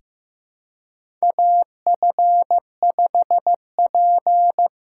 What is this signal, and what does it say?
Morse code, heard as a keyed tone, about 15 wpm, 700 Hz. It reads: AF5P
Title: K1USN SST QSO Practice - 15wpm